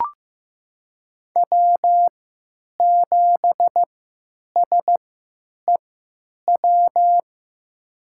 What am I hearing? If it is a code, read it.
Morse code, heard as a keyed tone, about 15 wpm, spaced as for 9 wpm, 700 Hz. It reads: W7SEW
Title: USA Call Signs - 5 Characters - 3x Character Spacing - 15wpm